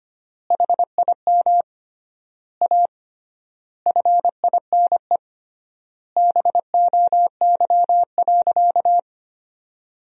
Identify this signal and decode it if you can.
Morse code, heard as a keyed tone, about 25 wpm, 700 Hz. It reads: HIM A FINE BOY.
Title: 3x Word-Spacing - Sentences from Top 500 Words - 25wpm